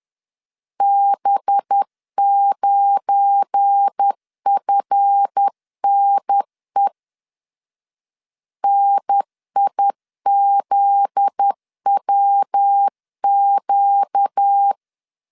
Morse code, heard as a keyed tone, about 10 wpm, 800 Hz. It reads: B9FNE NIZWQ
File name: LEZIONI / 8 YJ45 / 8 Y 10wpm 2615